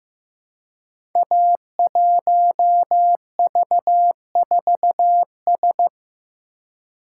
Morse code, heard as a keyed tone, about 15 wpm, 700 Hz. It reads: A1V4S
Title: Code Groups - 5 Letter:Numbers - 15wpm